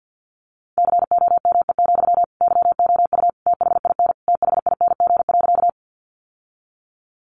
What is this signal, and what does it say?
Morse code, heard as a keyed tone, about 50 wpm, 700 Hz. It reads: COME, YOU THIN THING.
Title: Sentences from Top 1200 Words 50wpm